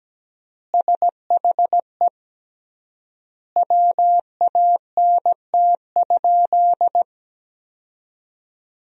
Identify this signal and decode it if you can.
Morse code, heard as a keyed tone, about 17 wpm, 700 Hz. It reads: SHE WANT?
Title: 3x Word-Spacing - Sentences from Top 200 Words - 17wpm